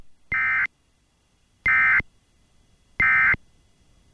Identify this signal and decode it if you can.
Emergency Alert System (SAME) tones, heard as harsh buzzing bursts. It is heard from 0.3 to 3.3 s.